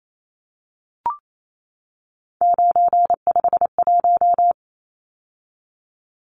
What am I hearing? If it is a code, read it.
Morse code, heard as a keyed tone, about 28 wpm, 700 Hz. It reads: 951
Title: Numbers - 3-Digit 28wpm